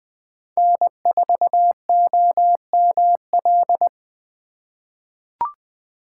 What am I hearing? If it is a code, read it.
Morse code, heard as a keyed tone, about 20 wpm, 700 Hz. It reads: N4OML
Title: USA Call Signs - 4x Character Spacing - 20wpm